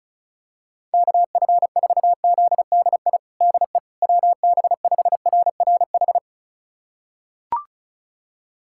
Morse code, heard as a keyed tone, about 34 wpm, 700 Hz. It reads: KF4ZDI DE WB5RRH
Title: USA Callsign DE Callsign - 6 Characters - 35wpm